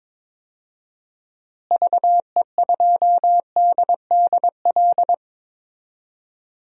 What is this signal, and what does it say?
Morse code, heard as a keyed tone, about 22 wpm, 700 Hz. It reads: VE2DDL
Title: DXCC-100 Call Signs - 6 Characters - Repeated 2x - 22wpm